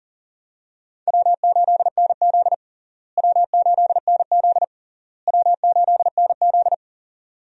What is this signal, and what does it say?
Morse code, heard as a keyed tone, about 40 wpm, 700 Hz. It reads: W8NZ W8NZ W8NZ
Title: USA Call Signs - 4 Characters - Repeated 3x - 40wpm